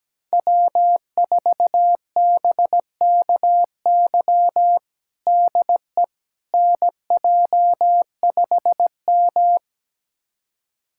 Morse code, heard as a keyed tone, about 17 wpm, 700 Hz. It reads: W4BKY DE NJ5M
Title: USA Callsign DE Callsign 17wpm